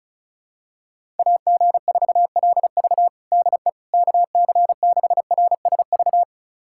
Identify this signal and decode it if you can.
Morse code, heard as a keyed tone, about 35 wpm, 700 Hz. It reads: AG4LV DE KC6RSV
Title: USA Callsign DE Callsign 35wpm